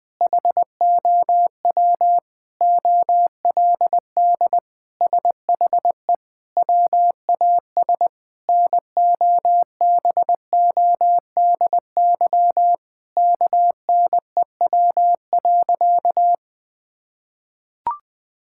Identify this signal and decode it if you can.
Morse code, heard as a keyed tone, about 20 wpm, 700 Hz. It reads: HOW OLD SHE WAS NOBODY KNEW.
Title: Sentences from Top 1600 Words 20wpm